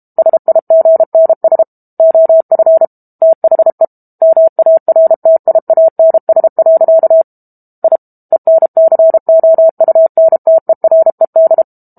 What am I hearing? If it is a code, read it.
Morse code, heard as a keyed tone, about 32 wpm, 650 Hz. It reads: SIGNS OF THE MARTIANS. I ENCOUNTERED
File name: Book Two / 06-Chapter 6 - 32 WPM